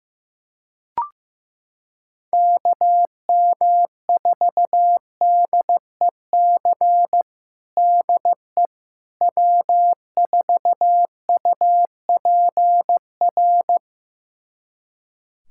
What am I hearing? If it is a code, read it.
Morse code, heard as a keyed tone, about 15 wpm, 700 Hz. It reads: KM4DEC DE W4UPR
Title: USA Callsign DE Callsign 15wpm